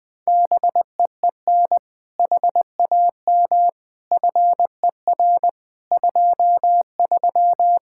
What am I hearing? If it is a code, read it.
Morse code, heard as a keyed tone, about 20 wpm, 700 Hz. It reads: BEEN HAM FER 23
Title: Speed-Racing Common QSO Phrases Encore - 20wpm